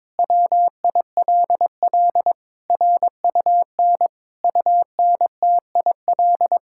Morse code, heard as a keyed tone, about 22 wpm, 700 Hz. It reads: WILL RUN UNTIL
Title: Sentences from Top 1000 Words - Review Part 5 - 22wpm